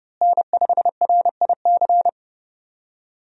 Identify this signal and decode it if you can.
Morse code, heard as a keyed tone, about 30 wpm, 700 Hz. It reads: N5RIC
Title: USA Call Signs - 5 Characters - Repeated 3x - 30wpm